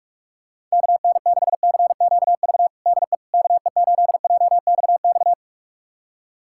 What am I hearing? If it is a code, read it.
Morse code, heard as a keyed tone, about 46 wpm, 700 Hz. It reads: KN6CQV DE KE8JXX
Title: USA Callsign DE Callsign - 6 Characters - 45wpm